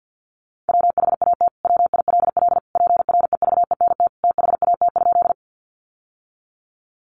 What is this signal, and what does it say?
Morse code, heard as a keyed tone, about 50 wpm, 700 Hz. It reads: WHAT WILL PREVENT THAT?
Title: Sentences from Top 1100 Words 50wpm